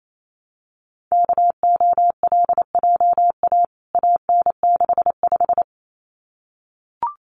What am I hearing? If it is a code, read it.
Morse code, heard as a keyed tone, about 28 wpm, 700 Hz. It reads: KOLJA AN65
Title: CWT Exchange 28wpm